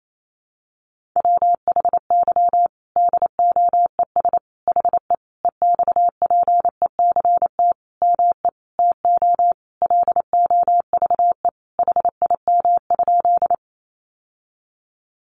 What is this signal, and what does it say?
Morse code, heard as a keyed tone, about 28 wpm, 700 Hz. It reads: WHY DOES HE EXPECT ME TO LOVE HIM?
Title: Sentences from Top 1100 Words 28wpm